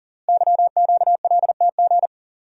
Morse code, heard as a keyed tone, about 38 wpm, 700 Hz. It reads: YQLTG
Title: Code Groups - 5 Letters - 40wpm